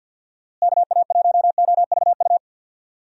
Morse code, heard as a keyed tone, about 49 wpm, 700 Hz. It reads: KA1OVU
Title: USA Call Signs - 6 Characters - Repeated 3x - 50wpm